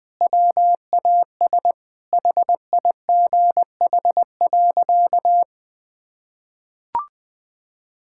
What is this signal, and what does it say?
Morse code, heard as a keyed tone, about 20 wpm, 700 Hz. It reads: WAS HIGH.
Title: Sentences with 5-Word Phrases - Spoken Beginning - 20wpm